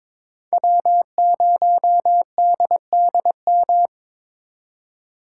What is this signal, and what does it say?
Morse code, heard as a keyed tone, about 26 wpm, 700 Hz. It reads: W0DDM
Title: Speed Racing - USA 1x3 Call Signs 22wpm